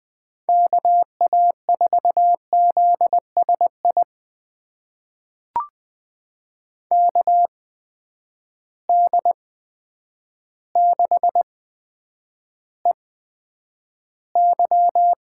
Morse code, heard as a keyed tone, about 20 wpm, spaced as for 9 wpm, 700 Hz. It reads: KA4ZSI KD6EY
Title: USA Call Signs - 6 Characters - 8x Character Spacing - 20wpm